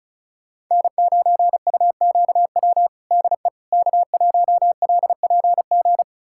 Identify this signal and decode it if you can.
Morse code, heard as a keyed tone, about 34 wpm, 690 Hz. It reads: N9UQW DE K1LPG
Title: USA Callsign DE Callsign - 5 Characters - 35wpm